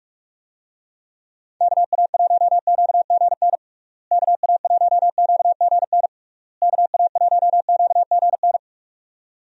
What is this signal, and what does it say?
Morse code, heard as a keyed tone, about 47 wpm, 700 Hz. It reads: KA1QGN KA1QGN KA1QGN
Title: USA Call Signs - 6 Characters - Repeated 3x - 45wpm